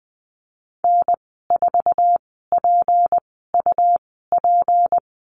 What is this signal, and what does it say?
Morse code, heard as a keyed tone, about 20 wpm, spaced as for 15 wpm, 700 Hz. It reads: N4PUP
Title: 200 Lesson 32g - Letter k - Call Signs 20wpm